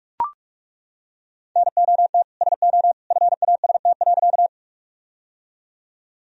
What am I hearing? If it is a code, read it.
Morse code, heard as a keyed tone, about 41 wpm, 700 Hz. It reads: NOT SO FAST.
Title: Sentences from Top 1000 Words - Review Part 2 - 45wpm